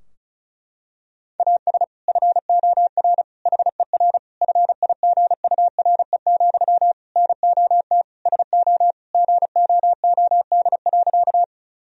Morse code, heard as a keyed tone, about 34 wpm, 700 Hz. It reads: AS FOR HER FIGURE, NOT SO GOOD.
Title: Sentences from Top 1000 Words - Review Part 2 - 35wpm